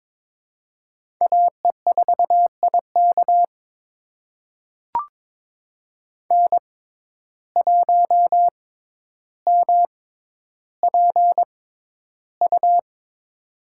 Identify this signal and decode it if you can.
Morse code, heard as a keyed tone, about 22 wpm, spaced as for 11 wpm, 700 Hz. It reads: AE4IK N1MPU
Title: USA Call Signs - 5 Characters - 6x Character Spacing - 22wpm